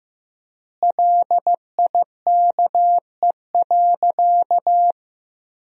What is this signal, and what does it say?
Morse code, heard as a keyed tone, about 15 wpm, 700 Hz. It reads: LIKE.
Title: 4x Word-Spacing - Sentences from Top 500 Words - 15wpm